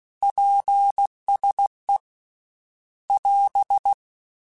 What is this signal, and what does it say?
Morse code, heard as a keyed tone, about 16 wpm, 800 Hz. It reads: PSE <AS>
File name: QSOs / QSOs-mp3s / WA7QOX 2@15